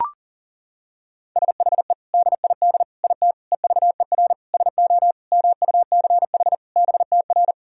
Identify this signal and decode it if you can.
Morse code, heard as a keyed tone, about 37 wpm, 700 Hz. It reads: SHE DID IT EVER SO MUCH BTR
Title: Sentences from Top 400 Words and Abbreviations 40wpm